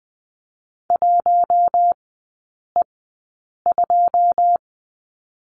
Morse code, heard as a keyed tone, about 19 wpm, spaced as for 8 wpm, 700 Hz. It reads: 1E2
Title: 121 Lesson 20e - Number 3 - Three Characters 20wpm